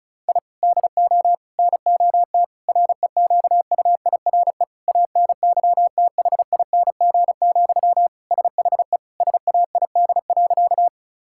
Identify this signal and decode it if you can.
Morse code, heard as a keyed tone, about 34 wpm, 700 Hz. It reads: I DO NOT REQUIRE ANYTHING, SHE SAID.
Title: Sentences from Top 1000 Words - Review Part 6 - 35wpm